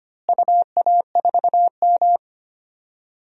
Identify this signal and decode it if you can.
Morse code, heard as a keyed tone, about 25 wpm, 700 Hz. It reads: UA4M
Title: DXCC-100 Call Signs - 4 Characters - Repeated 2x - 25wpm